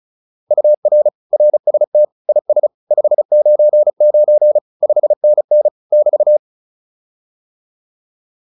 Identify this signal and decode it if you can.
Morse code, heard as a keyed tone, about 34 wpm, 600 Hz. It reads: UR RST IS 599 5NN <BT>